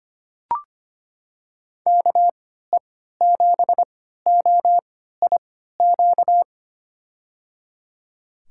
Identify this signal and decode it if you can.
Morse code, heard as a keyed tone, about 25 wpm, spaced as for 14 wpm, 690 Hz. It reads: KE7OIQ
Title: USA Call Signs - 6 Characters - 3x Character Spacing - 25wpm